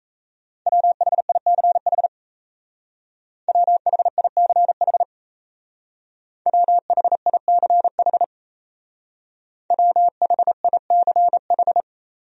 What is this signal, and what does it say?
Morse code, heard as a keyed tone, about 33 wpm, 700 Hz. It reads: WHICH WHICH WHICH WHICH
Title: Speed-Racing Top 100 Words 28wpm